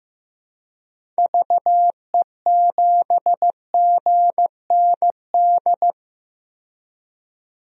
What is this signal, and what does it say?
Morse code, heard as a keyed tone, about 15 wpm, 700 Hz. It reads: VE7GND
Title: DXCC-100 Call Signs - 6 Characters - 15wpm